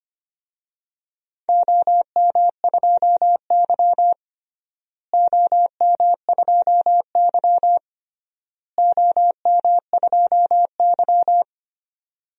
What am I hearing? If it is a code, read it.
Morse code, heard as a keyed tone, about 25 wpm, 700 Hz. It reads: OM2Y OM2Y OM2Y
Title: DXCC-100 Call Signs - 4 Characters - Repeated 3x - 25wpm